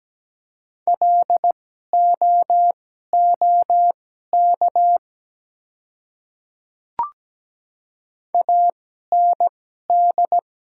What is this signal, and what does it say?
Morse code, heard as a keyed tone, about 17 wpm, spaced as for 12 wpm, 700 Hz. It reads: LOOK AND
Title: 2x Character Spacing - Top 100 Words - 17wpm